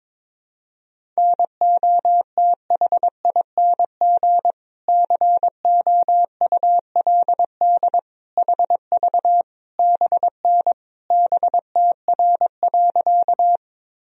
Morse code, heard as a keyed tone, about 22 wpm, 700 Hz. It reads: NOTHING COULD HV BN BTR.
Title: Sentences from Top 500 Words and Abbreviations 22wpm